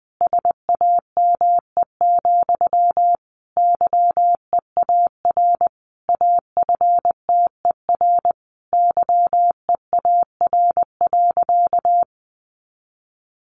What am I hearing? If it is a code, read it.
Morse code, heard as a keyed tone, about 20 wpm, 700 Hz. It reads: SAME, YEAR AFTER YEAR.
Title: Sentences from Top 300 Words 20wpm